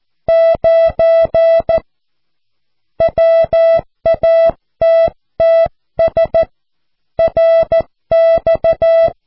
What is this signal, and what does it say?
Morse code, heard as a keyed tone, about 14 wpm, 660 Hz. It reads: 9 WATTS RX
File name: cw / QSO